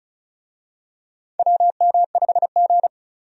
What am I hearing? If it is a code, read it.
Morse code, heard as a keyed tone, about 36 wpm, 700 Hz. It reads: WM5G
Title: USA Call Signs - 4 Characters - 35wpm